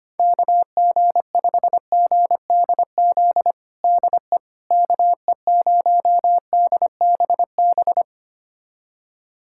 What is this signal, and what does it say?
Morse code, heard as a keyed tone, about 25 wpm, 700 Hz. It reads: KG5GDZ DE KE0DBB
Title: USA Callsign DE Callsign - 6 Characters - 25wpm